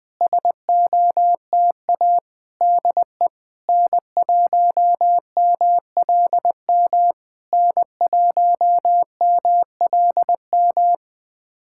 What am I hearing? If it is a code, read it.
Morse code, heard as a keyed tone, about 20 wpm, 700 Hz. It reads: SOTA DE N1MLM N1MLM
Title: SOTA QSO Practice 20wpm